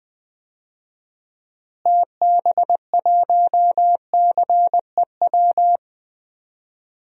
Morse code, heard as a keyed tone, about 20 wpm, 700 Hz. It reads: TB1CEW
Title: DXCC-100 Call Signs - 6 Characters - Repeated 2x - 20wpm